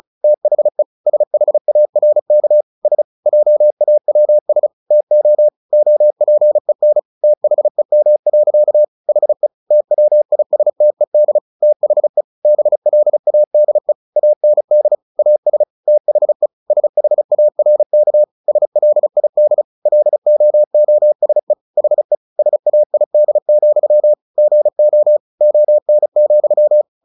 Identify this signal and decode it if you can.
Morse code, heard as a keyed tone, about 34 wpm, 600 Hz. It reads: THE SHARK S JAWS TO OPEN THEM. HE TWISTED THE BLADE AND AS THE SHARK SLID LOOSE HE SAID, GO ON,
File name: the old man and the sea 2 35wpm